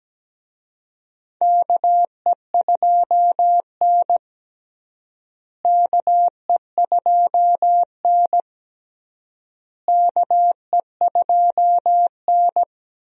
Morse code, heard as a keyed tone, about 17 wpm, 700 Hz. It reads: KE2N KE2N KE2N
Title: USA Call Signs - 4 Characters - Repeated 3x - 17wpm